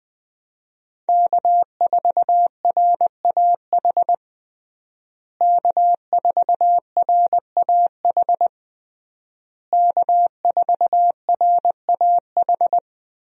Morse code, heard as a keyed tone, about 20 wpm, 700 Hz. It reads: K4RAH K4RAH K4RAH
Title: USA Call Signs - 5 Characters - Repeated 3x - 20wpm